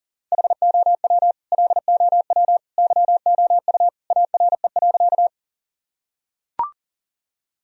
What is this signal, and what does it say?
Morse code, heard as a keyed tone, about 40 wpm, 700 Hz. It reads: HOW LOW YOU ARE.
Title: Sentences from Top 1000 Words - Review Part 1 - 40wpm